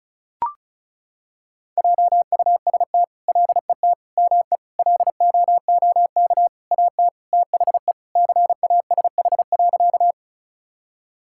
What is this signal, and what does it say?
Morse code, heard as a keyed tone, about 36 wpm, 700 Hz. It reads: JUST LET ME LOOK AT THE CASH.
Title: Sentences from Top 1000 Words - Review Part 5 - 35wpm